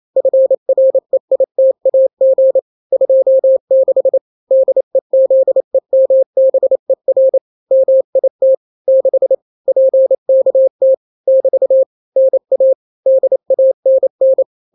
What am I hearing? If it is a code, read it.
Morse code, heard as a keyed tone, about 28 wpm, 530 Hz. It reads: FREITAG 26 DEZEMBER MIT 6 PKT <BT> NA DANN
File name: daily / 27wpm